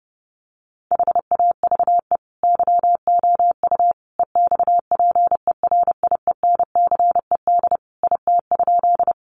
Morse code, heard as a keyed tone, about 30 wpm, 700 Hz. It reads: HAVE YOU EXPERIENCED IT?
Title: Sentences from Top 1100 Words 30wpm